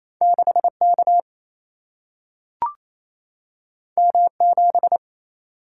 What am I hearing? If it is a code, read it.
Morse code, heard as a keyed tone, about 28 wpm, 700 Hz. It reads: 6K M7
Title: Single-Letter Single-Number 28wpm